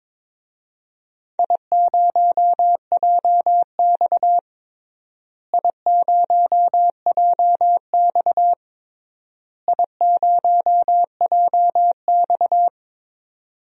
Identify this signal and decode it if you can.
Morse code, heard as a keyed tone, about 22 wpm, 700 Hz. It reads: I0JX I0JX I0JX
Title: DXCC-100 Call Signs - 4 Characters - Repeated 3x - 22wpm